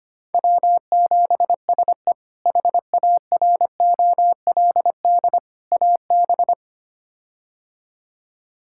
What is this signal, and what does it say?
Morse code, heard as a keyed tone, about 25 wpm, 700 Hz. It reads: W7SE HAROLD AB
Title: K1USN SST QSO Practice - 25wpm